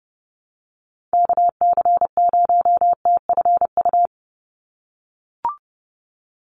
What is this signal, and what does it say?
Morse code, heard as a keyed tone, about 30 wpm, 700 Hz. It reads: KC0TFU
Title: USA Call Signs 30wpm